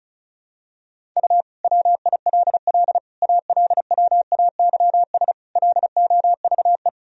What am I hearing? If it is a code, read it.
Morse code, heard as a keyed tone, about 34 wpm, 700 Hz. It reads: U WILL ALWAYS LOVE
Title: Sentences from Top 400 Words and Abbreviations 35wpm